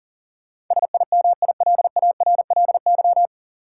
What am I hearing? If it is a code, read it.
Morse code, heard as a keyed tone, about 40 wpm, 700 Hz. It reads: SIMILARLY
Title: Sentences from Top 1400 Words - New Word Drill Alt - 40wpm